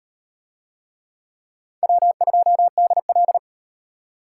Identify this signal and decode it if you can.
Morse code, heard as a keyed tone, about 34 wpm, 700 Hz. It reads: W2DL